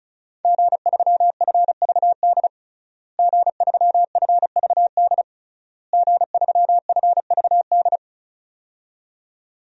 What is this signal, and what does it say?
Morse code, heard as a keyed tone, about 36 wpm, 700 Hz. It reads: G3FVD G3FVD G3FVD
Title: DXCC-100 Call Signs - 5 Characters - Repeated 3x - 35wpm